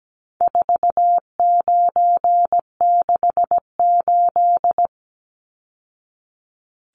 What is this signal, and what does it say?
Morse code, heard as a keyed tone, about 17 wpm, 700 Hz. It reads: 4968
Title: Numbers - 4-Digit 17wpm